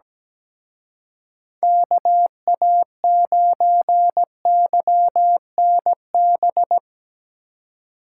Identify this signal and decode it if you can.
Morse code, heard as a keyed tone, about 17 wpm, 700 Hz. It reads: KA9YNB
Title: Speed Racing - USA 2x3 Call Signs 17wpm